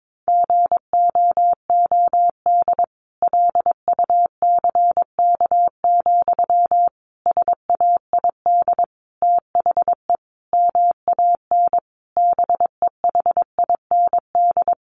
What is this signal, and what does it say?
Morse code, heard as a keyed tone, about 22 wpm, 700 Hz. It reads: GOOD LUCK, SAID THE MAN BEHIND
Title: A Sound Of Thunder 22wpm